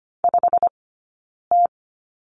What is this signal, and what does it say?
Morse code, heard as a keyed tone, about 25 wpm, 700 Hz. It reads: 5 T
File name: Lesson 17d - Letter u - Two Characters 25wpm